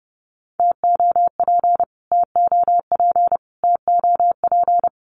Morse code, heard as a keyed tone, about 30 wpm, 700 Hz. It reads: TOP TOP TOP
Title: Top 500 Words Repeated 30wpm